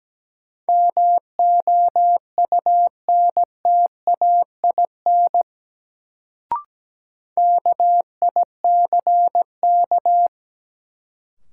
Morse code, heard as a keyed tone, about 17 wpm, 700 Hz. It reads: MOUNTAIN KICK
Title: Sentences from Top 1300 Words - New Word Drill Alt - 17wpm